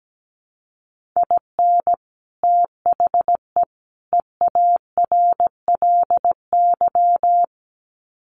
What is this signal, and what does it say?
Morse code, heard as a keyed tone, about 17 wpm, 700 Hz. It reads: IN THE EARLY
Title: Sets of 3 Words 17wpm